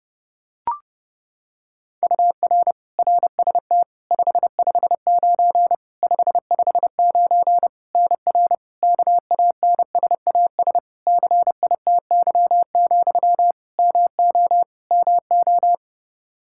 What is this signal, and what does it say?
Morse code, heard as a keyed tone, about 32 wpm, 700 Hz. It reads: UR RST 559 559 NR KANSAS CITY, MO MO
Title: K1USN SST QSO Practice 30wpm